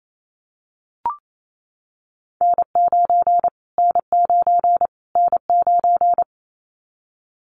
Morse code, heard as a keyed tone, about 28 wpm, 700 Hz. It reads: N9 N9 N9
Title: Callsign Prefixes for Top 25 DXCC - Repeated 3x - 28wpm